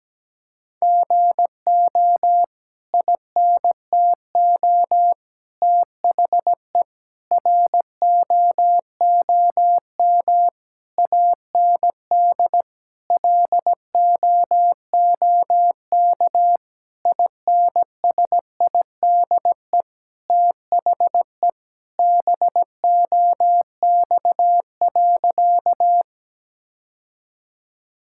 Morse code, heard as a keyed tone, about 17 wpm, 700 Hz. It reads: GO INTO THE ROOM AND LOOK INSIDE THE BOX.
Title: Sentences from Top 1000 Words - Review Part 6 - 17wpm